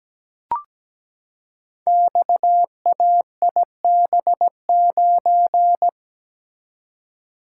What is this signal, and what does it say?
Morse code, heard as a keyed tone, about 17 wpm, 700 Hz. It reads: XAIB9
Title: Code Groups - 5 Letter:Numbers - 17wpm